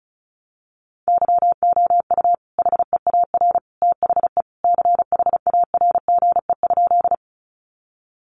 Morse code, heard as a keyed tone, about 34 wpm, 700 Hz. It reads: YOU HEAR THE CHARGE?
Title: Sentences from Top 900 Words 35wpm